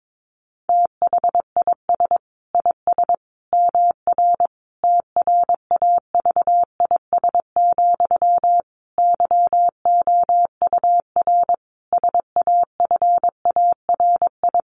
Morse code, heard as a keyed tone, about 22 wpm, 700 Hz. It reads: THIS IS MR TRAVIS, YOUR SAFARI
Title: A Sound Of Thunder 22wpm